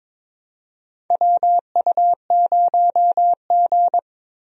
Morse code, heard as a keyed tone, about 22 wpm, 700 Hz. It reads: WU0G
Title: USA Call Signs - 4 Characters - Repeated 2x - 22wpm